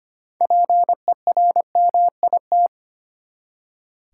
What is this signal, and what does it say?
Morse code, heard as a keyed tone, about 25 wpm, 700 Hz. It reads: PERMIT
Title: Six Letter Words - 25wpm